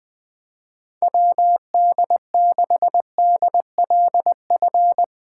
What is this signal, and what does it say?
Morse code, heard as a keyed tone, about 20 wpm, 700 Hz. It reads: WD6DLF
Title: USA Call Signs - 6x Character Spacing - 20wpm